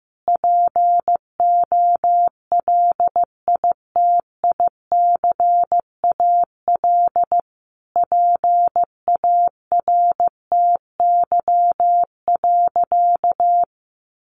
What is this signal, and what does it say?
Morse code, heard as a keyed tone, about 15 wpm, 700 Hz. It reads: POLITICAL PARTY.
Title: Sentences from Top 1400 Words 15wpm